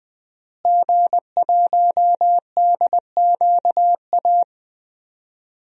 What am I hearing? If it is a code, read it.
Morse code, heard as a keyed tone, about 21 wpm, 700 Hz. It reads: G1DQA
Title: DXCC-100 Call Signs - 5 Characters - Repeated 2x - 20wpm